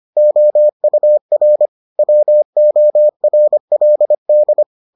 Morse code, heard as a keyed tone, about 25 wpm, 600 Hz. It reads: OUR WORLD